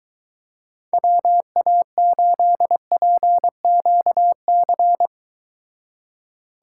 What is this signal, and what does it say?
Morse code, heard as a keyed tone, about 23 wpm, 700 Hz. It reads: WA8PQC